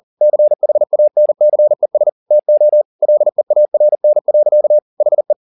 Morse code, heard as a keyed tone, about 40 wpm, 600 Hz. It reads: CHANCES TO LEARN. HE
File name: the old man and the sea 2 40wpm